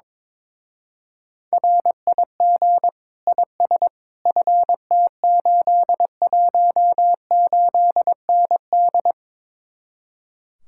Morse code, heard as a keyed tone, about 22 wpm, 700 Hz. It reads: RIG IS FT818ND
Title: Speed-Racing Common QSO Phrases - 22wpm